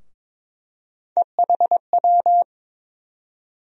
Morse code, heard as a keyed tone, about 21 wpm, 700 Hz. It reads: EHW
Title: USA Call Sign Postfixes - 3 Letters 22wpm